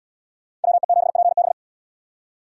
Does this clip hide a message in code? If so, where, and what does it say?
Yes: Morse code from 0.6 to 1.5 s, heard as a keyed tone, about 71 wpm, 700 Hz. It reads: WE7PB